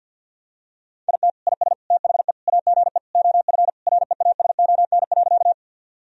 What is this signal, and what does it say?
Morse code, heard as a keyed tone, about 51 wpm, 700 Hz. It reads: IT IS THE AGE OF REASON.
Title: Sentences from Top 1000 Words - Review Part 3 - 50wpm